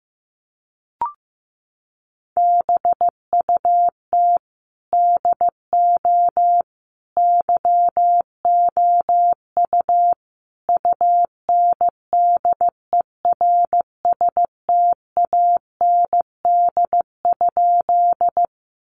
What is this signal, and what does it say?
Morse code, heard as a keyed tone, about 15 wpm, 700 Hz. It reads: BUT DO YOU UNDERSTAND?
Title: Sentences from Top 600 Words 15wpm